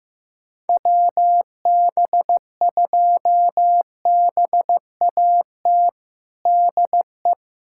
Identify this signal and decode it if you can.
Morse code, heard as a keyed tone, about 15 wpm, 700 Hz. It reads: WB2BAT DE
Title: USA Callsign DE Callsign - 6 Characters - 15wpm